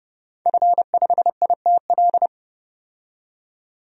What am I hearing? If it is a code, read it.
Morse code, heard as a keyed tone, about 29 wpm, 700 Hz. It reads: F5ITL
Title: DXCC-100 Call Signs - 5 Characters - 30wpm